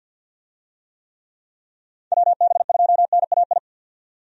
Morse code, heard as a keyed tone, about 49 wpm, 700 Hz. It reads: WB2NAI
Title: USA Call Signs - 6 Characters - 50wpm